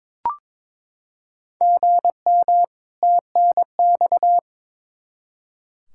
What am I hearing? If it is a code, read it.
Morse code, heard as a keyed tone, about 22 wpm, 700 Hz. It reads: GM TNX
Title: Practice QSO - Exchange 1 - Beginning - 22wpm